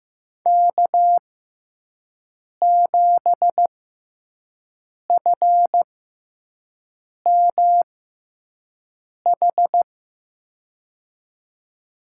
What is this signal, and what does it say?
Morse code, heard as a keyed tone, about 15 wpm, spaced as for 5 wpm, 700 Hz. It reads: K7FMH
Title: USA Call Signs - 5 Characters - 6x Character Spacing - 15wpm